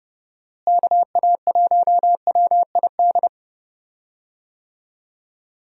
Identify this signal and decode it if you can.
Morse code, heard as a keyed tone, about 31 wpm, 700 Hz. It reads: KA1WID
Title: USA Call Signs - 6 Characters - Repeated 3x - 30wpm